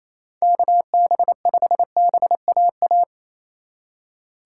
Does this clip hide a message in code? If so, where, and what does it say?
Yes: Morse code from 0.4 to 3.0 s, heard as a keyed tone, about 36 wpm, 700 Hz. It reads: KB5BAA